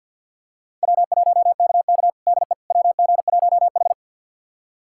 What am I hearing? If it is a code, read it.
Morse code, heard as a keyed tone, about 48 wpm, 700 Hz. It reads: W1KK DE WG1H
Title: USA Callsign DE Callsign - 4 Characters - 50wpm